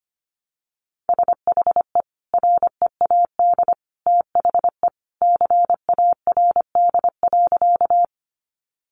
Morse code, heard as a keyed tone, about 25 wpm, 700 Hz. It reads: SHE READ THE CARD.
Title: Sentences from Top 600 Words 25wpm